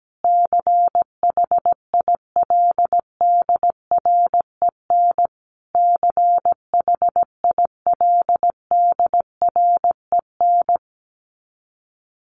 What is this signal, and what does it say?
Morse code, heard as a keyed tone, about 17 wpm, 700 Hz. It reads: CHILDREN CHILDREN
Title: Top 500 Words Repeated 17wpm